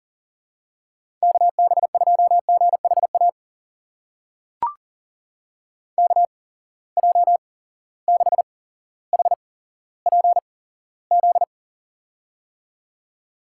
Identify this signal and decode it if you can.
Morse code, heard as a keyed tone, about 41 wpm, spaced as for 17 wpm, 700 Hz. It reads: KB2GHA KJ6HPZ
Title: USA Call Signs - 6 Characters - 8x Character Spacing - 40wpm